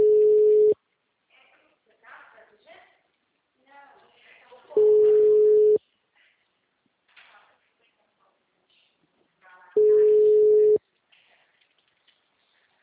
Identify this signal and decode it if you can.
Ringback tone (Europe, 425 Hz), with speech.